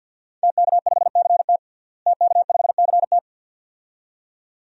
Morse code, heard as a keyed tone, about 50 wpm, 700 Hz. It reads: TK5CT TK5CT
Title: DXCC-100 Call Signs - 5 Characters - Repeated 2x - 50wpm